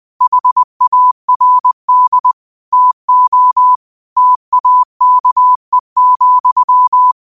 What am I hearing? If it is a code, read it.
Morse code, heard as a keyed tone, about 20 wpm, 1000 Hz. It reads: HARD TO TAKE,
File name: qotdfiles / 20wpm